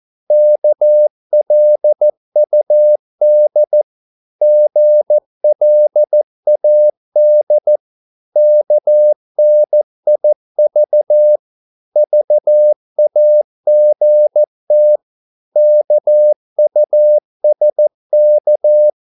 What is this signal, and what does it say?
Morse code, heard as a keyed tone, about 14 wpm, 600 Hz. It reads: KLUD GLAD KNIV VAGT KUSK
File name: Korte ord DK 14wpm